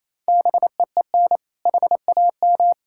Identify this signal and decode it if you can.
Morse code, heard as a keyed tone, about 28 wpm, 700 Hz. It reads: BEEN HAM
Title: K1USN SST QSO Practice 28wpm